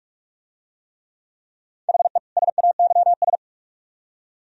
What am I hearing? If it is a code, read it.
Morse code, heard as a keyed tone, about 44 wpm, 700 Hz. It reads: HE SAYS